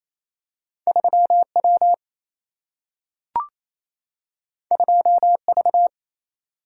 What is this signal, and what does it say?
Morse code, heard as a keyed tone, about 28 wpm, 700 Hz. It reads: 3 W 2 V
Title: Single-Letter Single-Number 28wpm